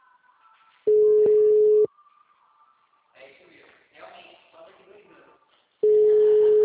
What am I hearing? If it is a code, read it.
Ringback tone (Europe, 425 Hz), with speech.